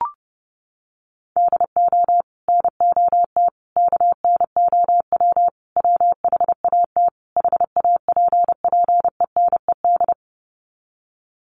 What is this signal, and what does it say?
Morse code, heard as a keyed tone, about 30 wpm, 700 Hz. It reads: DO NOT KNOW WHAT HAPPENED
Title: Sets of 5 Words Encore2 30wpm